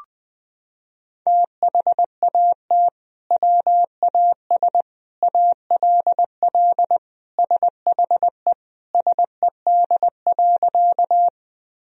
Morse code, heard as a keyed tone, about 20 wpm, 700 Hz. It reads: THAT WAS ALL SHE SED.
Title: Sentences from Top 100 Words and Abbreviations 20wpm